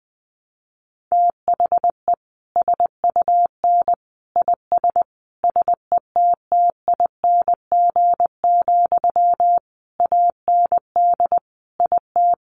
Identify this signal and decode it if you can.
Morse code, heard as a keyed tone, about 20 wpm, 700 Hz. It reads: THE SUN IS SETTING, AND IT
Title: Sentences from Top 1300 Words 20wpm